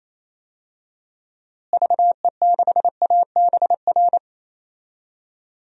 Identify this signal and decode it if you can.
Morse code, heard as a keyed tone, about 28 wpm, 700 Hz. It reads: VE6ABR
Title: DXCC-100 Call Signs - 6 Characters - Repeated 2x - 28wpm